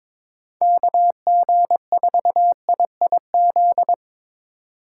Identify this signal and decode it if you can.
Morse code, heard as a keyed tone, about 22 wpm, 700 Hz. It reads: KG4IIZ
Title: USA Call Signs - 6 Characters - 3x Character Spacing - 22wpm